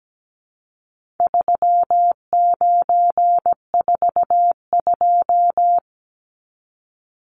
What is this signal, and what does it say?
Morse code, heard as a keyed tone, about 17 wpm, 700 Hz. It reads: 3942
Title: Numbers - 4-Digit 17wpm